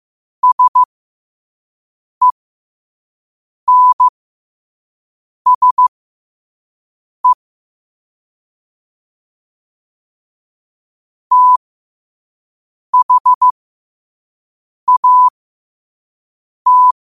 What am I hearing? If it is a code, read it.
Morse code, heard as a keyed tone, about 15 wpm, spaced as for 5 wpm, 1000 Hz. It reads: SENSE THAT
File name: qotdfiles / 5wpm